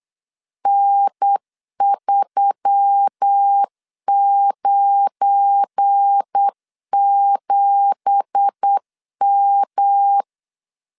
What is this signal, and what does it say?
Morse code, heard as a keyed tone, about 8 wpm, 800 Hz. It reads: N397M
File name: 8wpm